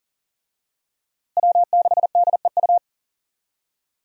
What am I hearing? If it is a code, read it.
Morse code, heard as a keyed tone, about 40 wpm, 700 Hz. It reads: W6DEU
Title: USA Call Signs - 5 Characters - Repeated 3x - 40wpm